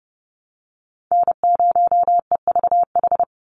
Morse code, heard as a keyed tone, about 30 wpm, 700 Hz. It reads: N0EVH
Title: NA SOTA Chaser Callsigns 30wpm